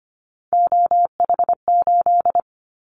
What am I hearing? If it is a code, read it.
Morse code, heard as a keyed tone, about 25 wpm, 700 Hz. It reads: OH8
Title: Callsign Prefixes for Top 25 DXCC - 25wpm